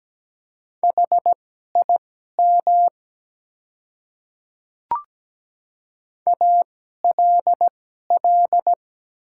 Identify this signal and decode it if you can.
Morse code, heard as a keyed tone, about 17 wpm, spaced as for 12 wpm, 700 Hz. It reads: HIM ALL
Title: 2x Character Spacing - Top 200 Words - 17wpm